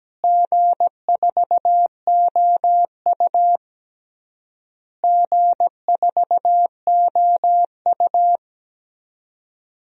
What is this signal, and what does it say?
Morse code, heard as a keyed tone, about 17 wpm, 700 Hz. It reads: G4OU G4OU
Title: DXCC-100 Call Signs - 4 Characters - Repeated 3x - 17wpm